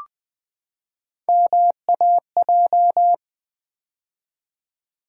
Morse code, heard as a keyed tone, about 20 wpm, 700 Hz. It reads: MAJ
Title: Build A Word - 3 Letters At A Time - 8-Letter Words - 20wpm